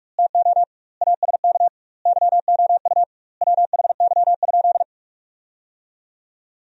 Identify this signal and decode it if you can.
Morse code, heard as a keyed tone, about 46 wpm, 700 Hz. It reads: TO ASK YOU WHY?
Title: 2x Word-Spacing - Sentences from Top 200 Words - 45wpm